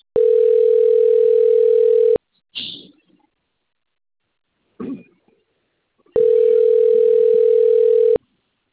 Ringback tone (North America), 440 + 480 Hz.